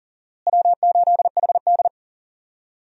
Morse code, heard as a keyed tone, about 37 wpm, 700 Hz. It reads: W8HD
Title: USA Call Signs - 4 Characters - Repeated 3x - 40wpm